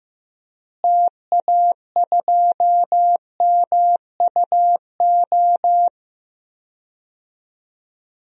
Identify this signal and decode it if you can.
Morse code, heard as a keyed tone, about 16 wpm, 700 Hz. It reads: TA2MUO
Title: DXCC-100 Call Signs - 6 Characters - Repeated 3x - 15wpm